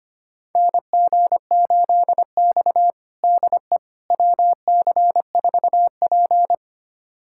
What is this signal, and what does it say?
Morse code, heard as a keyed tone, about 25 wpm, 700 Hz. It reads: NG8X DE WC4P
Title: USA Callsign DE Callsign - 4 Characters - 25wpm